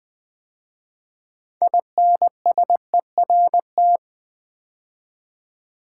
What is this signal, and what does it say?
Morse code, heard as a keyed tone, about 20 wpm, 700 Hz. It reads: INSERT
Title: 6 Letter Words - First 3 Letters Given - 20wpm